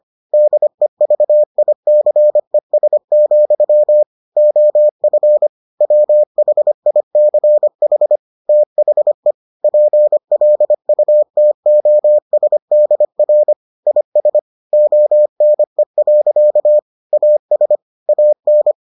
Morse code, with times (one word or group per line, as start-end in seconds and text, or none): 0.3-4.0 DEVICES,
4.4-5.5 OF
5.8-8.2 WHICH
8.5-9.3 THE
9.6-13.5 PLUTOSDR
13.9-14.4 IS
14.7-16.8 ONE.
17.1-17.8 AS
18.1-18.7 AN